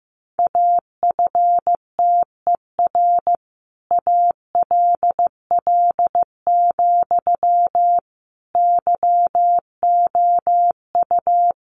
Morse code, heard as a keyed tone, about 15 wpm, 700 Hz. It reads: AFTER ALL, YOU
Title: Sentences from Top 600 Words 15wpm